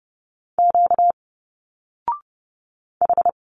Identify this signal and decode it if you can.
Morse code, heard as a keyed tone, about 30 wpm, 700 Hz. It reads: Q H
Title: Single Letters 30wpm